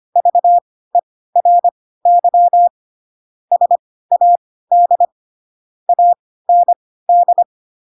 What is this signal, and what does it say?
Morse code, heard as a keyed tone, about 25 wpm, spaced as for 16 wpm, 700 Hz. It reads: VERY SAD AND
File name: Pinocchio / Pinocchio ch3 10 25 ik1hge